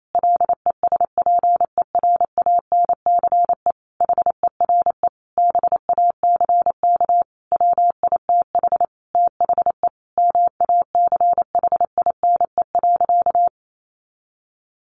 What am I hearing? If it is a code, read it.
Morse code, heard as a keyed tone, about 28 wpm, 700 Hz. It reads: LESPERANCE HERE BACK WITH THE MACHINE.
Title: A Sound Of Thunder 28wpm